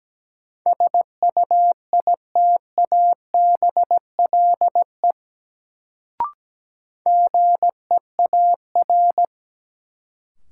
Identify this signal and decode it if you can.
Morse code, heard as a keyed tone, about 17 wpm, 700 Hz. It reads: SUITABLE GEAR
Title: Sentences from Top 1400 Words - New Word Drill Alt - 17wpm